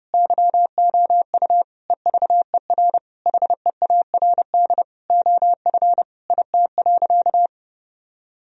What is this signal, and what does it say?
Morse code, heard as a keyed tone, about 30 wpm, 700 Hz. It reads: YOU EVER HEARD OF IT.
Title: Sentences from Top 1000 Words - Review Part 3 - 30wpm